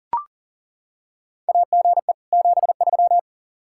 Morse code, heard as a keyed tone, about 40 wpm, 700 Hz. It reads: AGE 73
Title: Practice QSO - Exchange 3 - More Info - 40wpm